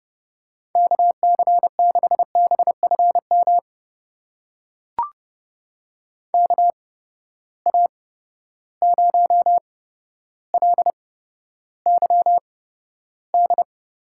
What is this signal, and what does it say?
Morse code, heard as a keyed tone, about 30 wpm, spaced as for 13 wpm, 700 Hz. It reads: KC6BFM KA0LYD
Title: USA Call Signs - 6 Characters - 8x Character Spacing - 30wpm